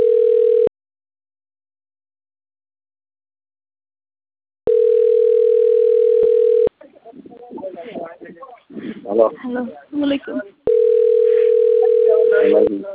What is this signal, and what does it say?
Ringback tone (North America), 440 + 480 Hz, with speech.
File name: autotestagent / 11-58-16.136..callcentertransfer. .e0dc93f8-07d8-44f9-a4ca